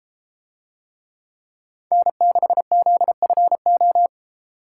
Morse code, heard as a keyed tone, about 32 wpm, 700 Hz. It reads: N6ZFO